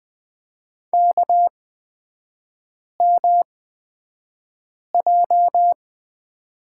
Morse code, heard as a keyed tone, about 20 wpm, spaced as for 5 wpm, 700 Hz. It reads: KMJ
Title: 236 Lesson 38e - Letter q - Three Characters 20wpm